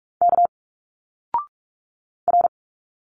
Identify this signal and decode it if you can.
Morse code, heard as a keyed tone, about 46 wpm, 700 Hz. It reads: K R
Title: Single Letters 45wpm